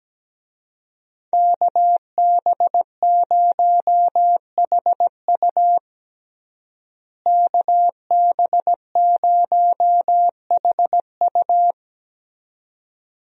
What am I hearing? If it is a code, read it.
Morse code, heard as a keyed tone, about 17 wpm, 700 Hz. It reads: KB0HU KB0HU
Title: USA Call Signs - 5 Characters - Repeated 2x - 17wpm